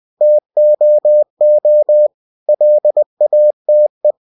Morse code, heard as a keyed tone, about 20 wpm, 600 Hz. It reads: TOO LATE
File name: cw-books / winnie-the-pooh / winnie-20wpm-page-0053